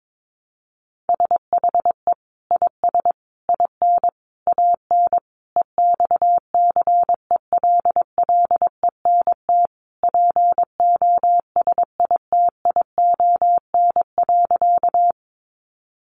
Morse code, heard as a keyed tone, about 22 wpm, 700 Hz. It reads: SHE IS IN AN EXCELLENT POSITION.
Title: Sentences from Top 1000 Words 22wpm